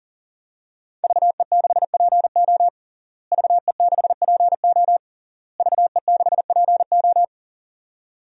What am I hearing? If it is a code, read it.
Morse code, heard as a keyed tone, about 40 wpm, 700 Hz. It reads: VE6PO VE6PO VE6PO
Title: DXCC-100 Call Signs - 5 Characters - Repeated 3x - 40wpm